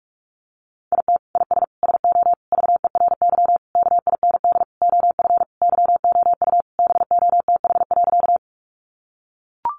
Morse code, heard as a keyed tone, about 44 wpm, 700 Hz. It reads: IT IS SO VERY KIND OF YOU BOTH.
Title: Sentences from Top 300 Words 45wpm